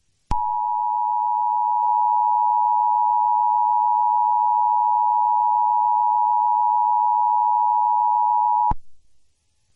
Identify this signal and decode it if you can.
Emergency Alert System (SAME) tones (the 853 and 960 Hz attention tone). It is heard from 0.3 to 8.8 s.
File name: u-s-eas-system-full